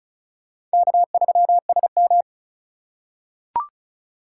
Morse code, heard as a keyed tone, about 35 wpm, 700 Hz. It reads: K3SM
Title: USA Call Signs - 8x Character Spacing - 35wpm